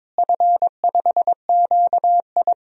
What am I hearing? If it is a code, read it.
Morse code, heard as a keyed tone, about 22 wpm, 700 Hz. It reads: F5QI
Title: DXCC-100 Call Signs - 4 Characters - Repeated 3x - 22wpm